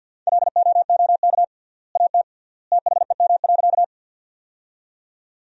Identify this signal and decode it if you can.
Morse code, heard as a keyed tone, about 53 wpm, 700 Hz. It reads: LOOK AT THEM.
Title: 3x Word-Spacing - Sentences from Top 100 Words - 50wpm